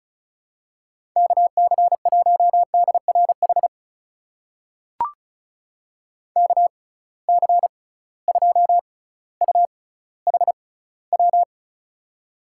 Morse code, heard as a keyed tone, about 34 wpm, spaced as for 18 wpm, 700 Hz. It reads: KC1DRH KC2UHW
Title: USA Call Signs - 6 Characters - 6x Character Spacing - 35wpm